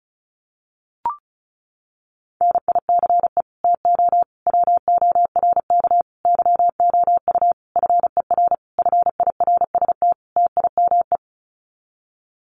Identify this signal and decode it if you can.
Morse code, heard as a keyed tone, about 34 wpm, 700 Hz. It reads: NICE TO WORK YOU FER FIRST TIME
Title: Common QSO Phrases 35wpm